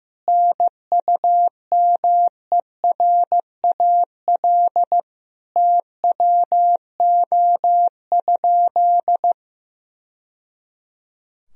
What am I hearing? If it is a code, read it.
Morse code, heard as a keyed tone, about 15 wpm, 700 Hz. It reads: NUMERAL TWO?
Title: Sentences from Top 1000 Words - Review Part 2 - 15wpm